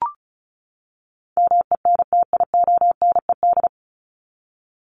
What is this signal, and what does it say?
Morse code, heard as a keyed tone, about 34 wpm, 700 Hz. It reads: MENTIONED
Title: ED Suffix - Top 100 Words with Base Words 35wpm